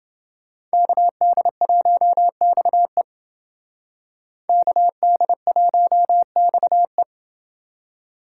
Morse code, heard as a keyed tone, about 29 wpm, 700 Hz. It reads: KD1XE KD1XE